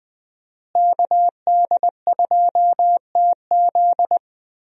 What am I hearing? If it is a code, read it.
Morse code, heard as a keyed tone, about 21 wpm, 700 Hz. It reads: KD2TZ
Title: USA Call Signs - 5 Characters - Repeated 3x - 20wpm